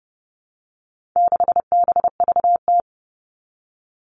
Morse code, heard as a keyed tone, about 29 wpm, 700 Hz. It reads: 6BVT
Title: All Cut Numbers - 4-Digit 30wpm